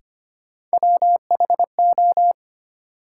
Morse code, heard as a keyed tone, about 25 wpm, 700 Hz. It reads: WHO
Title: Sets of 2 Words - New Word Drill - 25wpm